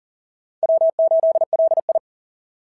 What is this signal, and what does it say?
Morse code, heard as a keyed tone, about 40 wpm, 650 Hz. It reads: W8LI
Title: SOTA QSO Practice 40wpm